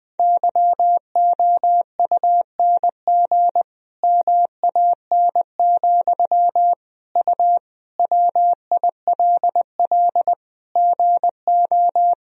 Morse code, heard as a keyed tone, about 20 wpm, 700 Hz. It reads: YOUNG MAN, U WILL GO
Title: Sentences from Top 400 Words and Abbreviations 20wpm